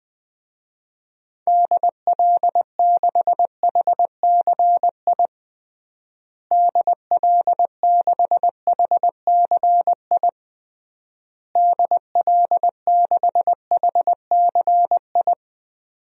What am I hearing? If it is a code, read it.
Morse code, heard as a keyed tone, about 20 wpm, 700 Hz. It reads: DL6HCI DL6HCI DL6HCI
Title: DXCC-100 Call Signs - 6 Characters - Repeated 3x - 20wpm